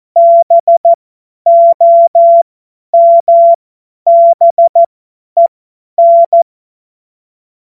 Morse code, heard as a keyed tone, about 14 wpm, spaced as for 10 wpm, 680 Hz. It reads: BOMBEN
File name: Morsekurs / Witze / Witz 05. April 2022 50 F70